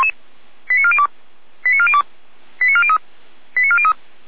Five-tone selective call (ZVEI), heard as quick runs of five tones, tones of about 70 ms. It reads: 89402 89402 89403 89403